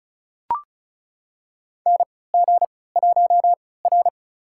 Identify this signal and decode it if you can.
Morse code, heard as a keyed tone, about 34 wpm, spaced as for 20 wpm, 700 Hz. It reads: NG1R
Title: USA Call Signs - 4 Characters - 3x Character Spacing - 35wpm